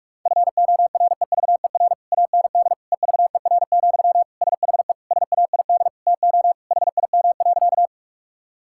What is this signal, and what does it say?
Morse code, heard as a keyed tone, about 46 wpm, 700 Hz. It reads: FOREVER AND EVER, SHE SAID TO HIM.
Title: Sentences from Top 1600 Words 45wpm